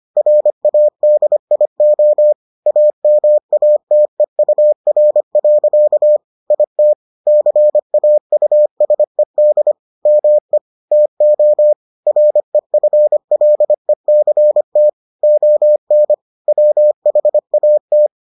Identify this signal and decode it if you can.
Morse code, heard as a keyed tone, about 25 wpm, 600 Hz. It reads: RADIO AMATEUR. IT CAUSED ME TO REFLECT ON WHAT